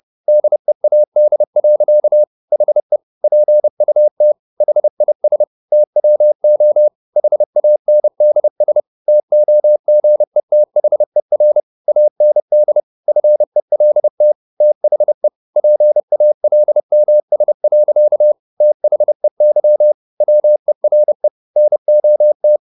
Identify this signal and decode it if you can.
Morse code, heard as a keyed tone, about 30 wpm, 600 Hz. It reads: DEAD. HE PUT HIS TWO HANDS TOGETHER AND FELT THE PALMS. THEY WERE NOT